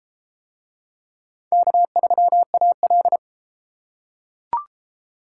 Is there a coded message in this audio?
Yes: Morse code from 1.5 to 3.2 s, heard as a keyed tone, about 32 wpm, 700 Hz. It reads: K3AL